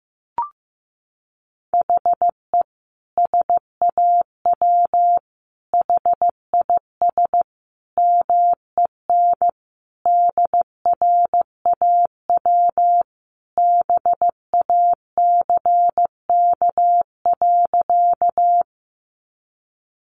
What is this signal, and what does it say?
Morse code, heard as a keyed tone, about 15 wpm, 700 Hz. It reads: HE SAW HIS MEN DRAW BACK.
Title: Sentences from Top 300 Words 15wpm